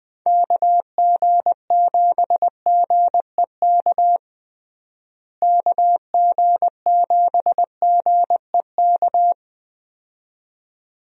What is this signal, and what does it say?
Morse code, heard as a keyed tone, about 20 wpm, 700 Hz. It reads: KG7GEK KG7GEK
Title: USA Call Signs - 6 Characters - Repeated 2x - 20wpm